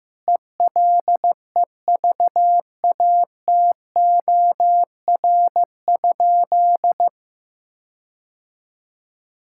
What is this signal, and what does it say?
Morse code, heard as a keyed tone, about 16 wpm, 700 Hz. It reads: ELEVATOR?
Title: Sentences from Top 1600 Words 15wpm